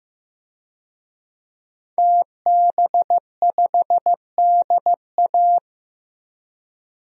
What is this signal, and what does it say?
Morse code, heard as a keyed tone, about 15 wpm, 700 Hz. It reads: TB5DA
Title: DXCC-100 Call Signs - 5 Characters - Repeated 2x - 15wpm